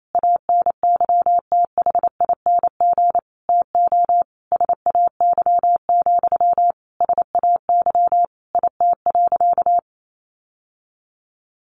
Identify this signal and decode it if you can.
Morse code, heard as a keyed tone, about 28 wpm, 700 Hz. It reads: ANYTHING TO SAY, SAY IT.
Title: Sentences from Top 1000 Words 28wpm